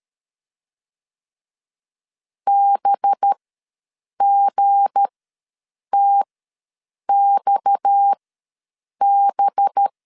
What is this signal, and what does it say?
Morse code, heard as a keyed tone, about 13 wpm, spaced as for 7 wpm, 800 Hz. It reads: BGTXB
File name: LEZIONI / 7 GF9Z / 7 G ws 12wpm 3140